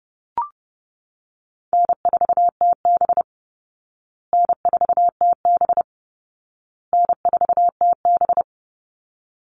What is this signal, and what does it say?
Morse code, heard as a keyed tone, about 28 wpm, 700 Hz. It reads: N4TB N4TB N4TB
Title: Call Signs Repeated 2x - 30wpm